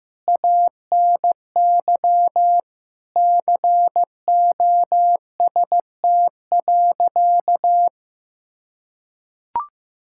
Morse code, heard as a keyed tone, about 15 wpm, 700 Hz. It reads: ANY COST.
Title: Sentences from Top 1000 Words - Review Part 5 - 15wpm